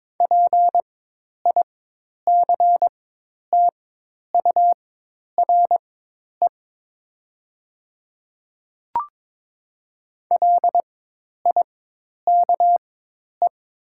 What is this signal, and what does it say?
Morse code, heard as a keyed tone, about 22 wpm, spaced as for 10 wpm, 700 Hz. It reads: PICTURE LIKE
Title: 4x Character Spacing - Top 200 Words - 22wpm